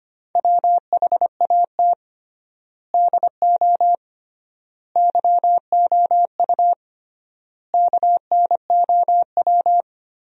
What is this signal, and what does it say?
Morse code, heard as a keyed tone, about 25 wpm, 700 Hz. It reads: WHAT DO YOU KNOW
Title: 3x Word-Spacing - Sentences from Top 100 Words - 25wpm